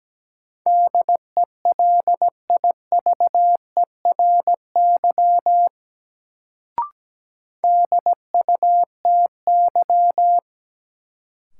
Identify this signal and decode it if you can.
Morse code, heard as a keyed tone, about 17 wpm, 700 Hz. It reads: DELIVERY DUTY
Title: Sentences from Top 1500 Words - New Word Drill Alt - 17wpm